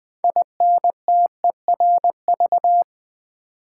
Morse code, heard as a keyed tone, about 20 wpm, 700 Hz. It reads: INTERV
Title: Build A Word - 3 Letters At A Time - 9-Letter Words - 20wpm